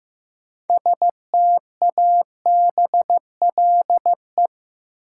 Morse code, heard as a keyed tone, about 15 wpm, 700 Hz. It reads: STABLE
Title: Sentences from Top 1500 Words - New Word Drill Alt - 15wpm